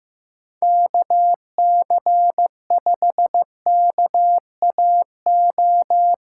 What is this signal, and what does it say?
Morse code, heard as a keyed tone, about 16 wpm, 700 Hz. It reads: KC5KAO
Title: USA Callsign DE Callsign 15wpm